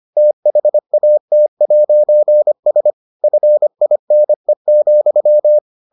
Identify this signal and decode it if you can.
Morse code, heard as a keyed tone, about 25 wpm, 600 Hz. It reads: THAT'S FINE,